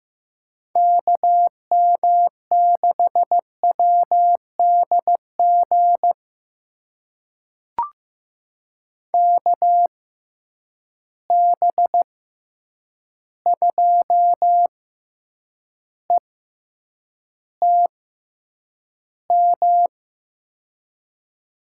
Morse code, heard as a keyed tone, about 16 wpm, spaced as for 8 wpm, 700 Hz. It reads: KM6WDG KB2ETM
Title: USA Call Signs - 6 Characters - 6x Character Spacing - 15wpm